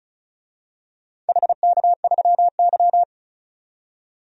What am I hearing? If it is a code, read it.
Morse code, heard as a keyed tone, about 34 wpm, 700 Hz. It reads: HK3Y
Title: DXCC-100 Call Signs - 4 Characters - 35wpm